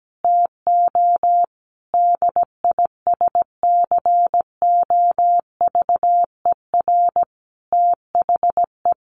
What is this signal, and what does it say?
Morse code, heard as a keyed tone, about 17 wpm, 700 Hz. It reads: TO DISCOVER THE
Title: Sentences from Top 1100 Words 17wpm